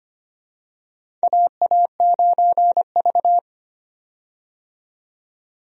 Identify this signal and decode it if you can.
Morse code, heard as a keyed tone, about 25 wpm, 700 Hz. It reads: AA9V
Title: USA Call Signs - 4 Characters - 25wpm